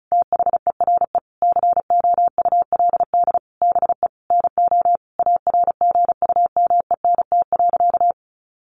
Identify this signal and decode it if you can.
Morse code, heard as a keyed tone, about 34 wpm, 700 Hz. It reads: THERE COULD BE NO ARGUMENT.
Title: A Sound Of Thunder 35wpm